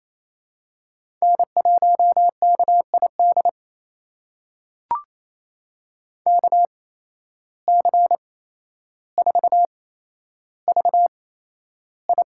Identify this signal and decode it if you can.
Morse code, heard as a keyed tone, about 28 wpm, spaced as for 12 wpm, 700 Hz. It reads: N1KID KC4VI
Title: USA Call Signs - 8x Character Spacing - 28wpm